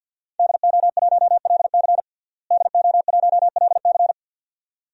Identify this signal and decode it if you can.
Morse code, heard as a keyed tone, about 52 wpm, 700 Hz. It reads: DO1LC DO1LC
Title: DXCC-100 Call Signs - 5 Characters - Repeated 2x - 50wpm